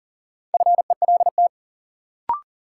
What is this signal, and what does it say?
Morse code, heard as a keyed tone, about 40 wpm, 700 Hz. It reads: FELT
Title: Sets of 5 Words Encore - New Word Drill - 40wpm